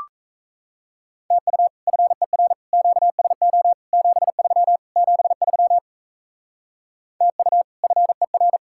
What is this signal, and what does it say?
Morse code, heard as a keyed tone, about 35 wpm, 700 Hz. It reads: TU FER QSO 73 73 TU FER
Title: Speed-Racing Common QSO Phrases - 28wpm